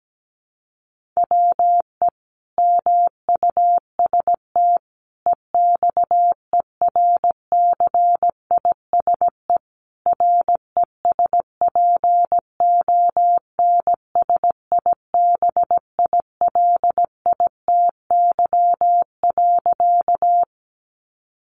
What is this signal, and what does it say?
Morse code, heard as a keyed tone, about 17 wpm, 700 Hz. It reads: WE MUST EXERCISE RESPONSIBILITY.
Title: Sentences from Top 1200 Words 17wpm